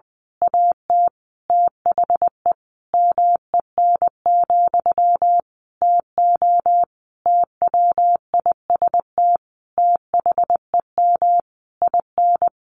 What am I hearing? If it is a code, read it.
Morse code, heard as a keyed tone, about 20 wpm, 700 Hz. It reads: AT THE MEN, TO TWIST THEM IN
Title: A Sound Of Thunder 20wpm